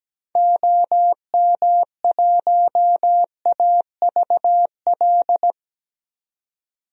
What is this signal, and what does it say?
Morse code, heard as a keyed tone, about 17 wpm, 700 Hz. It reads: OM1AVL
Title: DXCC-100 Call Signs - 6 Characters - 17wpm